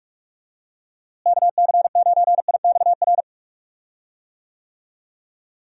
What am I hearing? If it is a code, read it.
Morse code, heard as a keyed tone, about 46 wpm, 700 Hz. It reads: KC9IXR
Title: USA Call Signs - 6 Characters - Repeated 3x - 45wpm